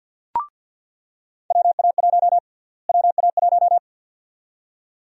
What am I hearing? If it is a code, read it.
Morse code, heard as a keyed tone, about 47 wpm, 700 Hz. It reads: WA1 WA1
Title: USA Call Sign Prefixes - Repeated 2x - 50wpm